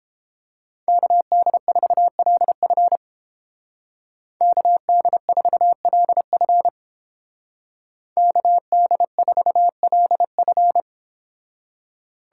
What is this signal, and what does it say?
Morse code, heard as a keyed tone, about 31 wpm, 700 Hz. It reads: KD4LF KD4LF KD4LF